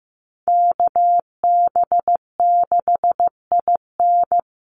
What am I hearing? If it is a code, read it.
Morse code, heard as a keyed tone, about 15 wpm, 700 Hz. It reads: KB6IN
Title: USA Call Signs 15wpm